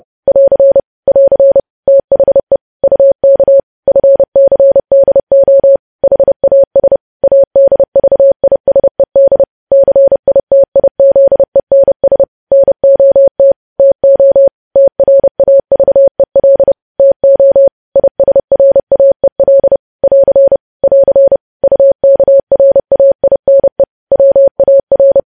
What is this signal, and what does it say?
Morse code, heard as a keyed tone, about 30 wpm, 550 Hz. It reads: <AR> <AR> THE UK FCDO HAS ADVISED CITIZENS NOT TO TRAVEL TO ISRAEL <AR> <AR> UKRAINE WAR
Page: news headlines in morse code at 30 wpm